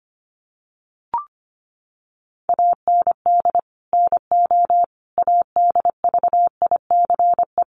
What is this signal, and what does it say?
Morse code, heard as a keyed tone, about 25 wpm, 700 Hz. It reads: AND NO ADVICE
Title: Sentences from Top 1100 Words 25wpm